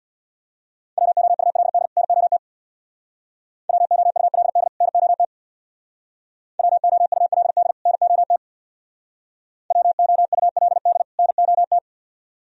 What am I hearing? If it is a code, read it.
Morse code, heard as a keyed tone, about 62 wpm, 700 Hz. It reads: WOULD NOT WOULD NOT WOULD NOT WOULD NOT
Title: Speed-Racing - Sets of 2 Words 50wpm